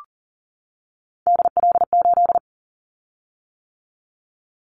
Morse code, heard as a keyed tone, about 40 wpm, 700 Hz. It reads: DL8
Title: Callsign Prefixes for Top 50 DXCC - 40wpm